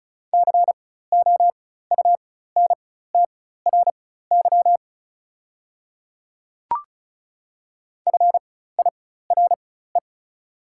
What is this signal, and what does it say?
Morse code, heard as a keyed tone, about 35 wpm, spaced as for 16 wpm, 690 Hz. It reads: COUNTRY FIRE
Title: 4x Character Spacing - Top 500 Words - 35wpm